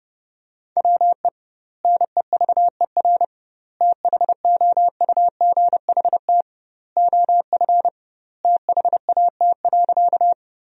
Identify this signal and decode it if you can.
Morse code, heard as a keyed tone, about 30 wpm, 700 Hz. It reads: WE NEVER THOUGHT OF THAT.
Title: 2x Word-Spacing - Sentences from Top 300 Words - 30wpm